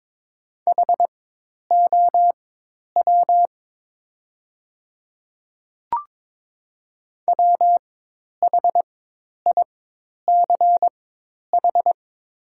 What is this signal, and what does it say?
Morse code, heard as a keyed tone, about 22 wpm, spaced as for 10 wpm, 700 Hz. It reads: HOW WHICH
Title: 4x Character Spacing - Top 100 Words - 22wpm